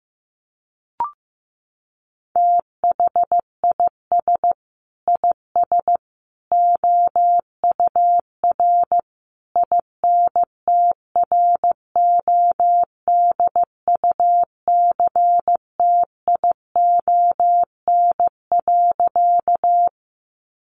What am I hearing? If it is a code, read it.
Morse code, heard as a keyed tone, about 15 wpm, 700 Hz. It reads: THIS IS OUR INTRODUCTION.
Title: Sentences from Top 1000 Words 15wpm